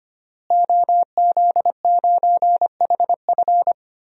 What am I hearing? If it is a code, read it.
Morse code, heard as a keyed tone, about 25 wpm, 700 Hz. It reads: OZ9HF
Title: DXCC-100 Call Signs - 5 Characters - Repeated 3x - 25wpm